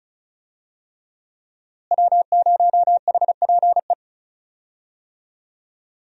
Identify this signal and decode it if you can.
Morse code, heard as a keyed tone, about 34 wpm, 700 Hz. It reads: W0HPE
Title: USA Call Signs - 5 Characters - Repeated 3x - 35wpm